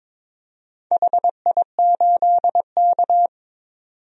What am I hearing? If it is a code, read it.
Morse code, heard as a keyed tone, about 23 wpm, 700 Hz. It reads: HI8K
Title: DXCC-100 Call Signs - 4 Characters - 22wpm